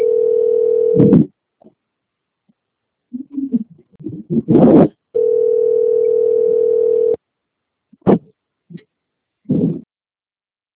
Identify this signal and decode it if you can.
Ringback tone (North America), 440 + 480 Hz.